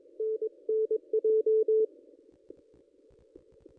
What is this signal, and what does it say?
Morse code, heard as a keyed tone, about 23 wpm, 440 Hz. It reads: NNJ
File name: contests / ss-cw-k5tr-op-n5rz -1800